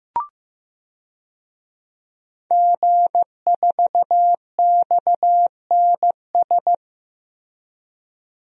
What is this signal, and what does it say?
Morse code, heard as a keyed tone, about 15 wpm, 700 Hz. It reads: G4XNS
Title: DXCC-100 Call Signs - 5 Characters - Repeated 2x - 15wpm